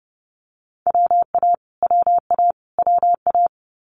Morse code, heard as a keyed tone, about 30 wpm, 700 Hz. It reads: WA WA WA
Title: Multi-letter Phonemes Repeated 3x 30wpm